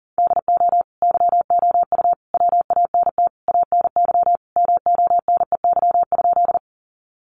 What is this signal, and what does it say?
Morse code, heard as a keyed tone, about 40 wpm, 700 Hz. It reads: DO YOU WANT ANY MONEY?
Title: Sentences from Top 400 Words 40wpm